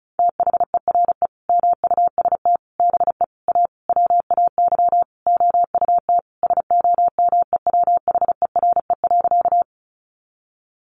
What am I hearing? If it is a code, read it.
Morse code, heard as a keyed tone, about 34 wpm, 700 Hz. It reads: THERE MUST BE A WAY OUT SOMEWHERE.
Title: Sentences from Top 1400 Words 35wpm